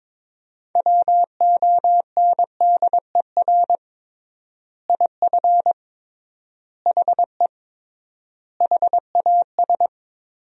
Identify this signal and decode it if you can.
Morse code, heard as a keyed tone, about 22 wpm, 700 Hz. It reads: WONDER IF HE HAS
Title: 3x Word-Spacing - Sentences from Top 500 Words - 22wpm